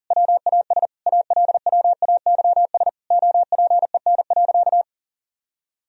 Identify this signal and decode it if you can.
Morse code, heard as a keyed tone, about 40 wpm, 700 Hz. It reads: WAS ALWAYS OPEN.
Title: Sentences from Top 1000 Words - Review Part 1 - 40wpm